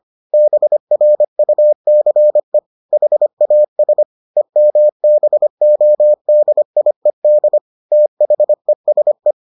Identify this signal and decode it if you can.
Morse code, heard as a keyed tone, about 25 wpm, 600 Hz. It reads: BRUCE HAS EMBODIED THESE